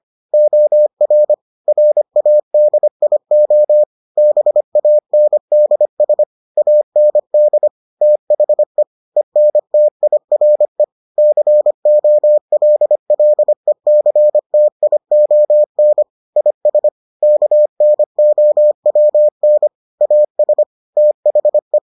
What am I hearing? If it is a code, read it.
Morse code, heard as a keyed tone, about 25 wpm, 600 Hz. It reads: OR RADIO BANDS AND THE ENTIRE COLLECTION IS KNOWN AS THE